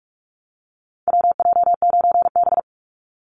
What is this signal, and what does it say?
Morse code, heard as a keyed tone, about 44 wpm, 700 Hz. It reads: WJ9B
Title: Call Signs - 45wpm